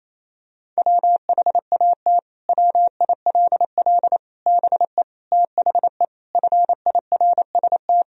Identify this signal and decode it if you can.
Morse code, heard as a keyed tone, about 28 wpm, 700 Hz. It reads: WHAT WILL BE THE FIRST
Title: Sentences from Top 1000 Words - Review Part 2 - 28wpm